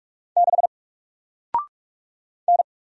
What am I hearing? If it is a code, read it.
Morse code, heard as a keyed tone, about 44 wpm, 700 Hz. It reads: 6 N
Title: Single Letter-Number Alt 45wpm